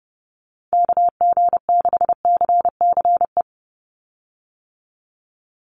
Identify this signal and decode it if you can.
Morse code, heard as a keyed tone, about 30 wpm, 700 Hz. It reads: KG6CCE
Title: USA Call Signs 30wpm